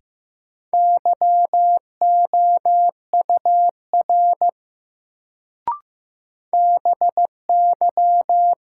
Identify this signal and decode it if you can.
Morse code, heard as a keyed tone, about 15 wpm, 700 Hz. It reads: YOUR BY
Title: Sentences from Top 100 Words - New Word Drill - 15wpm